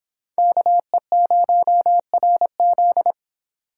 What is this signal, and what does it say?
Morse code, heard as a keyed tone, about 32 wpm, 700 Hz. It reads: KE0RZ